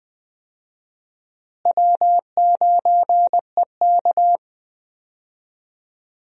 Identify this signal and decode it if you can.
Morse code, heard as a keyed tone, about 20 wpm, 700 Hz. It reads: W9EK
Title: USA Call Signs - 4 Characters - Repeated 2x - 20wpm